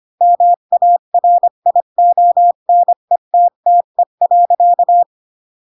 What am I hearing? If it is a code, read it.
Morse code, heard as a keyed tone, about 25 wpm, 700 Hz. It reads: MARIONETTE.
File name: Pinocchio / Pinocchio ch8 20 25 ik1hge